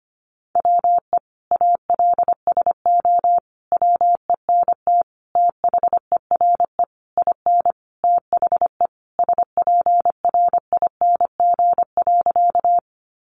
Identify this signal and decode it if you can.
Morse code, heard as a keyed tone, about 25 wpm, 700 Hz. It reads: WE ALSO WENT THERE IN THE SPRING.
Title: Sentences from Top 1200 Words 25wpm